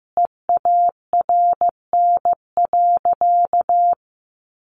Morse code, heard as a keyed tone, about 15 wpm, 700 Hz. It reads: EARN.
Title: Sentences from Top 1500 Words 15wpm